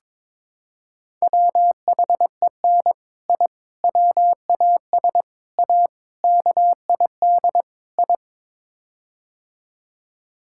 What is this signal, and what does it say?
Morse code, heard as a keyed tone, about 21 wpm, 700 Hz. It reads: WHEN I WAS A KID I
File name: practice / 22wpm / Sets of 6 Words - 22wpm